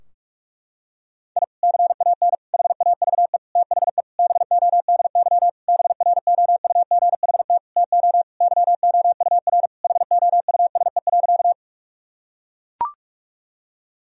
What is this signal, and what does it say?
Morse code, heard as a keyed tone, about 41 wpm, 700 Hz. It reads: I CAN HAVE THE BODY BROUGHT TO YOUR HOUSE.
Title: Sentences from Top 1000 Words - Review Part 2 - 45wpm